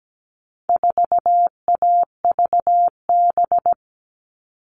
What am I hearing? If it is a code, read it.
Morse code, heard as a keyed tone, about 17 wpm, 700 Hz. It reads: 4AVB
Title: All Cut Numbers - 4-Digit 17wpm